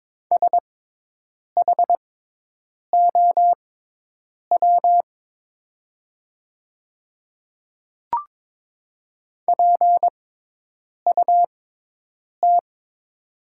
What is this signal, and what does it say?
Morse code, heard as a keyed tone, about 22 wpm, spaced as for 8 wpm, 700 Hz. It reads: SHOW PUT
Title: 6x Character Spacing - Top 200 Words - 22wpm